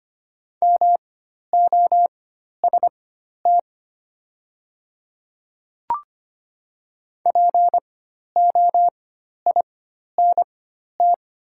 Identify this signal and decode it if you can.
Morse code, heard as a keyed tone, about 25 wpm, spaced as for 12 wpm, 700 Hz. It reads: MOST POINT
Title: 4x Character Spacing - Top 200 Words - 25wpm